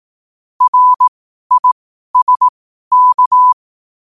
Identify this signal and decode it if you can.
Morse code, heard as a keyed tone, about 18 wpm, spaced as for 12 wpm, 1000 Hz. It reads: RISK
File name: qotdfiles / 12wpm